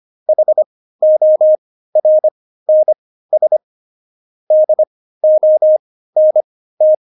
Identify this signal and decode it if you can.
Morse code, heard as a keyed tone, about 25 wpm, spaced as for 15 wpm, 620 Hz. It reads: HORNS DONT
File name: Jokes-25-char-15-eff